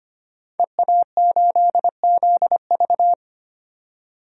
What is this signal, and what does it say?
Morse code, heard as a keyed tone, about 25 wpm, 700 Hz. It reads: EA8ZV
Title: DXCC-100 Call Signs - 5 Characters - 25wpm